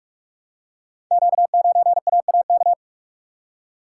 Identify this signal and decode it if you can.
Morse code, heard as a keyed tone, about 44 wpm, 690 Hz. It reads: Q9AAK
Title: Code Groups - 5 Letter:Numbers - 45wpm